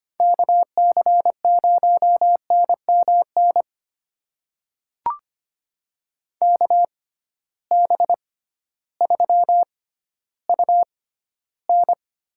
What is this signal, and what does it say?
Morse code, heard as a keyed tone, about 25 wpm, spaced as for 13 wpm, 700 Hz. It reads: KC0NMN KB3UN
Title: USA Call Signs - 6 Characters - 6x Character Spacing - 25wpm